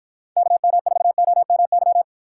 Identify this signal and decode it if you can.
Morse code, heard as a keyed tone, about 54 wpm, 700 Hz. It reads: KM4OMY